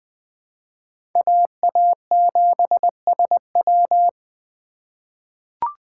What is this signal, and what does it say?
Morse code, heard as a keyed tone, about 21 wpm, 700 Hz. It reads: AA7SW
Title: USA Call Signs - 5 Characters - 2x Character Spacing - 20wpm